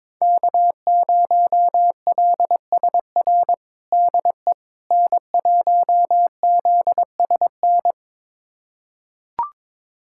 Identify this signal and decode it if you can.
Morse code, heard as a keyed tone, about 22 wpm, 700 Hz. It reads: K0LSR DE N1ZSN
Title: USA Callsign DE Callsign - 5 Characters - 22wpm